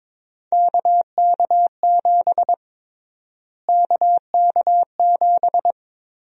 Morse code, heard as a keyed tone, about 22 wpm, 700 Hz. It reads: KK7 KK7
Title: USA Call Sign Prefixes - Repeated 2x - 22wpm